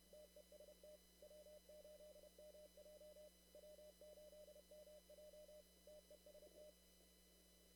Morse code, heard as a keyed tone, about 30 wpm, 600 Hz, with noise. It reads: TEST W8MJ W8MJ TEST
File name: CW / kt5j wpx cw-op-k5pi